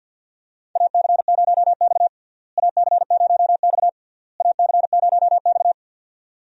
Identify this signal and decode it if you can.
Morse code, heard as a keyed tone, about 50 wpm, 700 Hz. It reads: AC0X AC0X AC0X
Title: USA Call Signs - 4 Characters - Repeated 3x - 50wpm